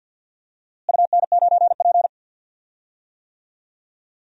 Morse code, heard as a keyed tone, about 53 wpm, 700 Hz. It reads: UN9P
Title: DXCC-100 Call Signs - 4 Characters - Repeated 2x - 50wpm